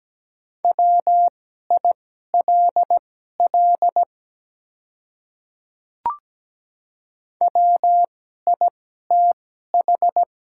Morse code, heard as a keyed tone, about 17 wpm, spaced as for 12 wpm, 700 Hz. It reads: WILL WITH
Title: 2x Character Spacing - Top 100 Words - 17wpm